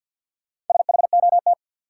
Morse code, heard as a keyed tone, about 52 wpm, 700 Hz. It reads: SHOT
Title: Sentences from Top 1100 Words - New Word Drill Alt - 50wpm